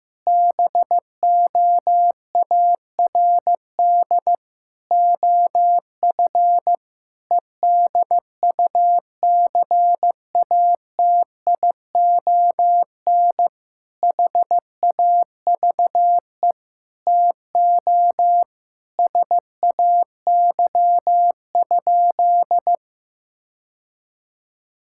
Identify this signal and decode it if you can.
Morse code, heard as a keyed tone, about 15 wpm, 700 Hz. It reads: BOARD OF EDUCATION HAVE TO SAY?
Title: Sentences from Top 1000 Words - Review Part 6 - 15wpm